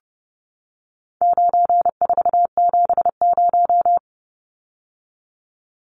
Morse code, heard as a keyed tone, about 30 wpm, 700 Hz. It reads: 9470
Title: Numbers - 4-Digit 30wpm